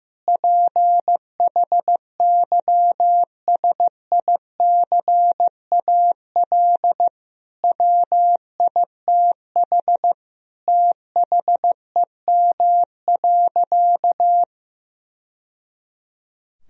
Morse code, heard as a keyed tone, about 15 wpm, 700 Hz. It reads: PHYSICAL WITH THEM.
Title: Sentences from Top 1000 Words - Review Part 4 - 15wpm